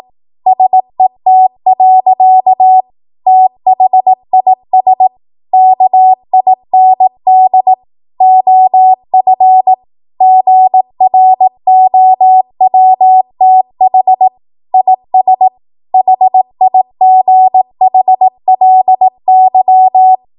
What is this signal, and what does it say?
Morse code, heard as a keyed tone, about 18 wpm, 750 Hz. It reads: SET. THIS KIND OF GROWTH IS HIGHLY
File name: Archive / 18 WPM / 18WPM